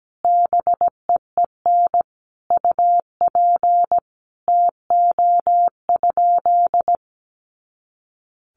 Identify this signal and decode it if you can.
Morse code, heard as a keyed tone, about 17 wpm, 700 Hz. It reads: BEEN UP TO?
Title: Sentences from Top 100 Words 17wpm